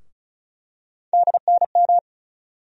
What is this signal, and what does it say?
Morse code, heard as a keyed tone, about 34 wpm, 700 Hz. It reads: DNM
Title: USA Call Sign Postfixes - 3 Letters 35wpm